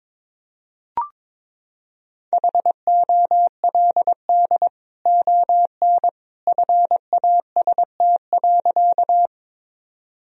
Morse code, heard as a keyed tone, about 22 wpm, 700 Hz. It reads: HOLD ON FAST.
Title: Sentences from Top 1000 Words - Review Part 2 - 22wpm